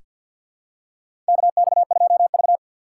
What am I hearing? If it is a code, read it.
Morse code, heard as a keyed tone, about 47 wpm, 700 Hz. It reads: KX2V
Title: USA Call Signs - 3x Character Spacing - 50wpm